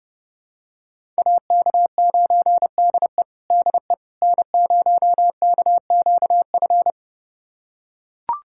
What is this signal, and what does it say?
Morse code, heard as a keyed tone, about 29 wpm, 700 Hz. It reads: AK9DE DE N0KQF
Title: USA Callsign DE Callsign - 5 Characters - 30wpm